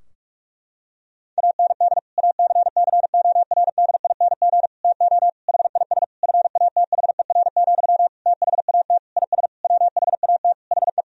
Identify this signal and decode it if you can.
Morse code, heard as a keyed tone, about 46 wpm, 700 Hz. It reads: AND ACCORDING TO HIS FATHER, THAT IS WHAT HE
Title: Sentences from Top 1000 Words - Review Part 4 - 45wpm